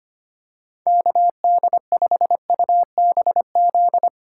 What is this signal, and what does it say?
Morse code, heard as a keyed tone, about 25 wpm, 700 Hz. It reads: KD5UBZ
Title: USA Call Signs - 6 Characters - 2x Character Spacing - 25wpm